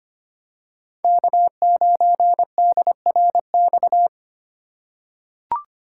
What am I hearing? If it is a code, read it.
Morse code, heard as a keyed tone, about 25 wpm, 700 Hz. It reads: K9DRX
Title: USA Call Signs - 5 Characters - 4x Character Spacing - 25wpm